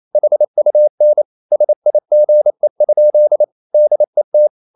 Morse code, heard as a keyed tone, about 28 wpm, 600 Hz. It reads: HUN SIGE? DET
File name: Ferie QRQ 28wpm